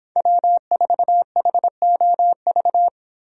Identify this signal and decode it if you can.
Morse code, heard as a keyed tone, about 26 wpm, 700 Hz. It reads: W4HOV